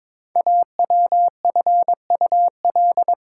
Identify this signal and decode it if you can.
Morse code, heard as a keyed tone, about 22 wpm, 700 Hz. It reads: AWFUL
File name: 5 Letter Words - First 3 Letters Given - 22wpm